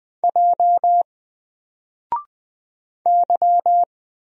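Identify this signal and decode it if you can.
Morse code, heard as a keyed tone, about 20 wpm, 700 Hz. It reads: J Y
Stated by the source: Single letter and number Morse drill at 20 wpm, from N4LCD.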